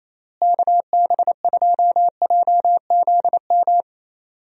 Morse code, heard as a keyed tone, about 28 wpm, 700 Hz. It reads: KB2JZM
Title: USA Call Signs - 6 Characters - Repeated 3x - 28wpm